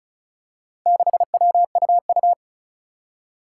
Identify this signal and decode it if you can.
Morse code, heard as a keyed tone, about 36 wpm, 700 Hz. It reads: 6WUU
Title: Four Challenging Letter-Number Combinations 35wpm